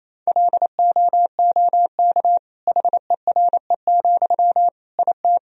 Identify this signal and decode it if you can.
Morse code, heard as a keyed tone, about 28 wpm, 700 Hz. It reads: LOOK HERE, IT
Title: Sentences from Top 1000 Words - Review Part 2 - 28wpm